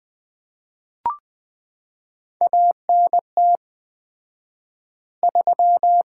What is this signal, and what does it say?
Morse code, heard as a keyed tone, about 20 wpm, 700 Hz. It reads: ANT 3
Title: Common QSO Phrases - 4x Word-Spacing - 20wpm